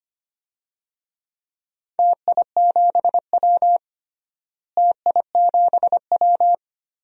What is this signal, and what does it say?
Morse code, heard as a keyed tone, about 25 wpm, 700 Hz. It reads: TI7W TI7W
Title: DXCC-100 Call Signs - 4 Characters - Repeated 3x - 25wpm